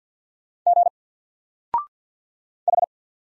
Morse code, heard as a keyed tone, about 48 wpm, 700 Hz. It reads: G H
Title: Single Letter-Number 50wpm